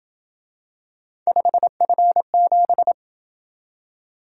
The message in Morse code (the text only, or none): 5F7